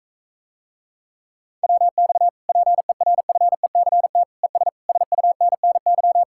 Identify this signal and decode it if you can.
Morse code, heard as a keyed tone, about 45 wpm, 700 Hz. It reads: WX PERFECT ES SUNNY